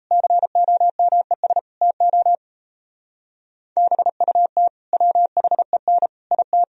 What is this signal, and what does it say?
Morse code, heard as a keyed tone, about 34 wpm, 700 Hz. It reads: COMES TO BUT WHEN IT
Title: Speed-Racing - Sets of 5 Words 28wpm